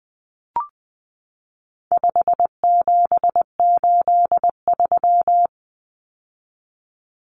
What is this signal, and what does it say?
Morse code, heard as a keyed tone, about 20 wpm, 700 Hz. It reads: 5783
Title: Numbers - 4-Digit 20wpm